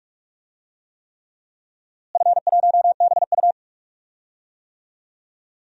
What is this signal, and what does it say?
Morse code, heard as a keyed tone, about 46 wpm, 690 Hz. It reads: F1BU
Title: DXCC-100 Call Signs - 4 Characters - Repeated 3x - 45wpm